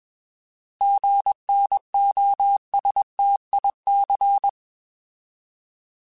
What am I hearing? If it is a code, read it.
Morse code, heard as a keyed tone, about 21 wpm, 800 Hz. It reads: GNOSTIC
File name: Alphabetical / Alphabetical mp3s / Gs @20